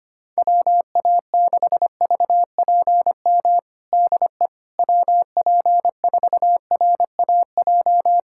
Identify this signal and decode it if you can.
Morse code, heard as a keyed tone, about 25 wpm, 700 Hz. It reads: WA6VPM DE WP4RAJ
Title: USA Callsign DE Callsign - 6 Characters - 25wpm